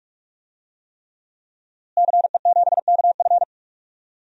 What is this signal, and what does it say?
Morse code, heard as a keyed tone, about 46 wpm, 700 Hz. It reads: CE7KF
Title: DXCC-100 Call Signs - 5 Characters - Repeated 3x - 45wpm